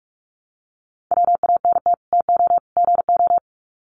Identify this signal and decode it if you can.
Morse code, heard as a keyed tone, about 44 wpm, 700 Hz. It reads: WANT TO GO
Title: Sets of 3 Words 45wpm